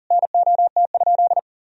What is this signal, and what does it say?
Morse code, heard as a keyed tone, about 37 wpm, 700 Hz. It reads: NOT?
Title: Sentences from Top 1000 Words - Review Part 3 - 40wpm